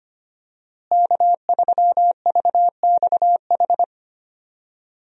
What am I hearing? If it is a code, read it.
Morse code, heard as a keyed tone, about 25 wpm, 700 Hz. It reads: K3VXH
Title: USA Call Signs - 5 Characters - Repeated 2x - 25wpm